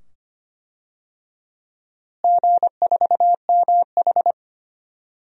Morse code, heard as a keyed tone, about 25 wpm, 700 Hz. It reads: G4MH
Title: DXCC-100 Call Signs - 4 Characters - Repeated 2x - 25wpm